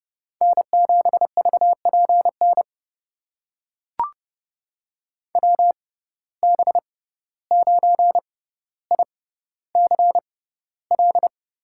Morse code, heard as a keyed tone, about 30 wpm, spaced as for 15 wpm, 700 Hz. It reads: N7VPN WB9ICL
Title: USA Call Signs - 6x Character Spacing - 30wpm